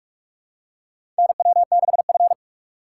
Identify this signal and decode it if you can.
Morse code, heard as a keyed tone, about 41 wpm, 700 Hz. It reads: NW6F
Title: USA Call Signs - 4x Character Spacing - 45wpm